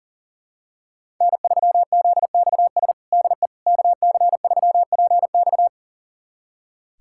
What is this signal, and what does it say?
Morse code, heard as a keyed tone, about 40 wpm, 700 Hz. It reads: N3ZXS DE KC3PX
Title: USA Callsign DE Callsign - 5 Characters - 40wpm